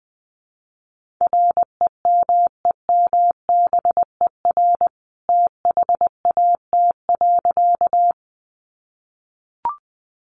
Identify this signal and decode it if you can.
Morse code, heard as a keyed tone, about 20 wpm, 700 Hz. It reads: REMEMBER THAT.
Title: A Sound Of Thunder 20wpm